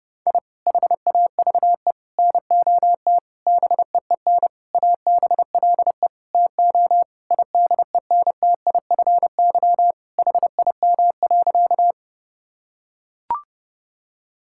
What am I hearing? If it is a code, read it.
Morse code, heard as a keyed tone, about 29 wpm, 700 Hz. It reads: I HAVE NOT BEEN ABLE TO IDENTIFY HIM.
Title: Sentences from Top 1000 Words - Review Part 6 - 30wpm